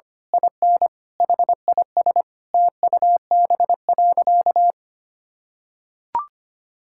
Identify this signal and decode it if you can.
Morse code, heard as a keyed tone, about 25 wpm, 700 Hz. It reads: IN HIS TUB.
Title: Pangram Challenge v2 - 25wpm